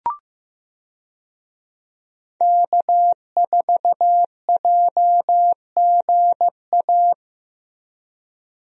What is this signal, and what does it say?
Morse code, heard as a keyed tone, about 15 wpm, 700 Hz. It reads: K4JGA
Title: USA Call Signs - 5 Characters - Repeated 3x - 15wpm